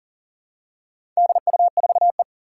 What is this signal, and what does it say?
Morse code, heard as a keyed tone, about 40 wpm, 700 Hz. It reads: DU4E
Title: Four Challenging Letter-Number Combinations 40wpm